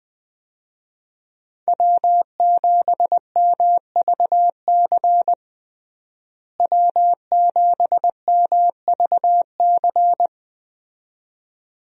Morse code, heard as a keyed tone, about 20 wpm, 700 Hz. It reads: W7MVC W7MVC
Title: USA Call Signs - 5 Characters - Repeated 2x - 20wpm